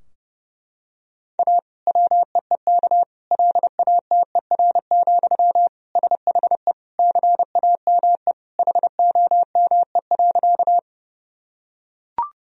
Morse code, heard as a keyed tone, about 30 wpm, 700 Hz. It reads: A WEEK LATER, SHE CAME HOME.
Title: Sentences from Top 1000 Words - Review Part 4 - 30wpm